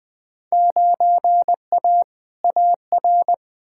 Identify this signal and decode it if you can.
Morse code, heard as a keyed tone, about 21 wpm, 700 Hz. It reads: 9A AR
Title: Field Day Exchanges Encore - 20wpm